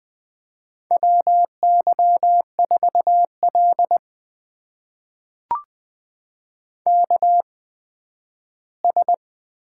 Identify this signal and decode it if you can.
Morse code, heard as a keyed tone, about 20 wpm, spaced as for 12 wpm, 700 Hz. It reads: WY4L KS
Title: USA Call Signs - 4 Characters - 8x Character Spacing - 20wpm